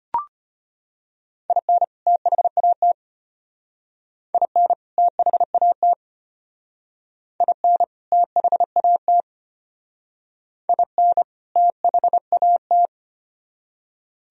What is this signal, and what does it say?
Morse code, heard as a keyed tone, about 33 wpm, 700 Hz. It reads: IN THAT IN THAT IN THAT IN THAT